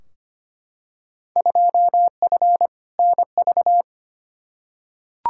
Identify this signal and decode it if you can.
Morse code, heard as a keyed tone, about 25 wpm, 700 Hz. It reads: 2F NV
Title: Field Day Exchanges - 25wpm